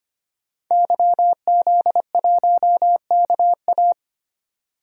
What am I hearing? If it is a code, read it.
Morse code, heard as a keyed tone, about 25 wpm, 700 Hz. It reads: YZ1KA
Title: DXCC-100 Call Signs - 5 Characters - 25wpm